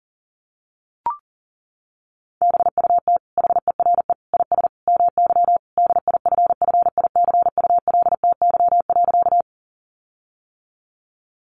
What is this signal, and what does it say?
Morse code, heard as a keyed tone, about 40 wpm, 700 Hz. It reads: BUT HERE IS MY DIFFICULTY.
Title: Sentences from Top 1500 Words 40wpm